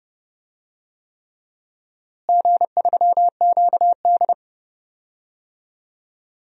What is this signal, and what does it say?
Morse code, heard as a keyed tone, about 30 wpm, 700 Hz. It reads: G3QD
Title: DXCC-100 Call Signs - 4 Characters - Repeated 3x - 30wpm